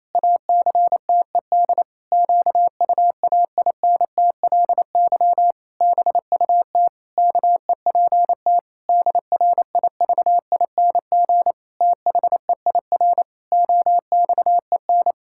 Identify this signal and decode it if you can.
Morse code, heard as a keyed tone, about 28 wpm, 700 Hz. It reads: ACTED QUAINTLY BUT KEPT DRIVING THEIR OXEN
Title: Pangram Challenge v2 - 28wpm